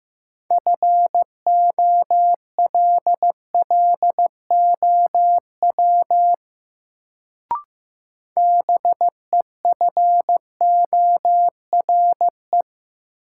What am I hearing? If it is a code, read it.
Morse code, heard as a keyed tone, about 15 wpm, 700 Hz. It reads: FOLLOW BEFORE
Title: Sentences from Top 200 Words - New Word Drill - 15wpm